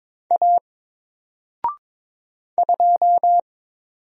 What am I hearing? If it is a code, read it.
Morse code, heard as a keyed tone, about 22 wpm, 700 Hz. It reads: A 2
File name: Single-Letter-Number-22wpm